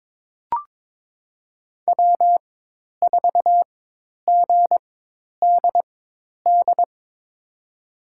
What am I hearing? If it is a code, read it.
Morse code, heard as a keyed tone, about 22 wpm, spaced as for 10 wpm, 700 Hz. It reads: W4GDD
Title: USA Call Signs - 4x Character Spacing - 22wpm